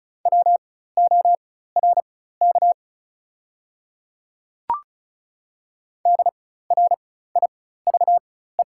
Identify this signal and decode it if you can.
Morse code, heard as a keyed tone, about 36 wpm, spaced as for 16 wpm, 690 Hz. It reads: WORK DRIVE
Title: 4x Character Spacing - Top 500 Words - 35wpm